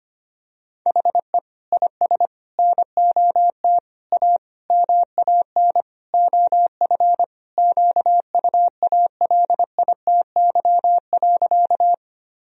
Morse code, heard as a keyed tone, about 25 wpm, 700 Hz. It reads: HE IS NOT A MAN OF QUALITY.
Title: Sentences from Top 1000 Words - Review Part 4 - 25wpm